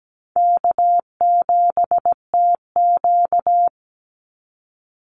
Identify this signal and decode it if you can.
Morse code, heard as a keyed tone, about 17 wpm, 700 Hz. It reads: K7TQ
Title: NA SOTA Chaser Callsigns 17wpm